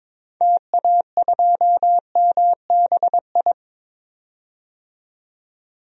Morse code, heard as a keyed tone, about 22 wpm, 700 Hz. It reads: TA2MBI
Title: DXCC-100 Call Signs - 6 Characters - 22wpm